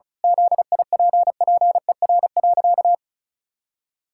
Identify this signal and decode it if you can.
Morse code, heard as a keyed tone, about 34 wpm, 700 Hz. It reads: ZIPPER.
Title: Pangram Challenge v2 - 35wpm